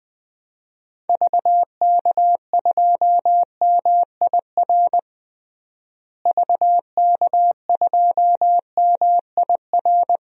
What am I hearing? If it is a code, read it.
Morse code, heard as a keyed tone, about 20 wpm, 700 Hz. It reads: VK2MIR VK2MIR
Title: DXCC-100 Call Signs - 6 Characters - Repeated 2x - 20wpm